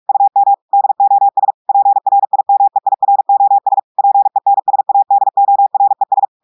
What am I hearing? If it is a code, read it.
Morse code, heard as a keyed tone, about 46 wpm, 800 Hz. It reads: UM DOS PRIMEIROS PENSADORES
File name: audios / portugais / portugais texte en clair 1 vitesse 45 mots minutes